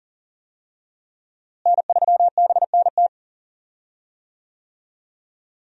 Morse code, heard as a keyed tone, about 40 wpm, 700 Hz. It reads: N3BNT
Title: USA Call Signs - 5 Characters - Repeated 2x - 40wpm